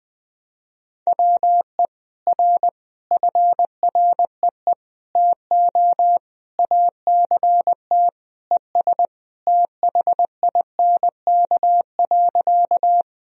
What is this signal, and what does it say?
Morse code, heard as a keyed tone, about 21 wpm, 700 Hz. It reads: WE R FREE TO ACT ES THINK.
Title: Sentences from Top 500 Words and Abbreviations 20wpm